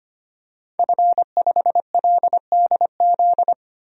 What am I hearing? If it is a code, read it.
Morse code, heard as a keyed tone, about 25 wpm, 700 Hz. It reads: F5LDZ
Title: DXCC-100 Call Signs - 5 Characters - 25wpm